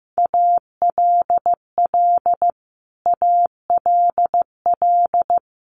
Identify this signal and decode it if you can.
Morse code, heard as a keyed tone, about 15 wpm, 700 Hz. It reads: ALL ALL
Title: Top 300 Words Repeated 15wpm